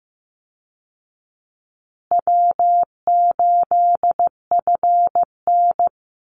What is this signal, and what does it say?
Morse code, heard as a keyed tone, about 15 wpm, 700 Hz. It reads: W8FN
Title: Easy Call Signs Repeated 3x - 15wpm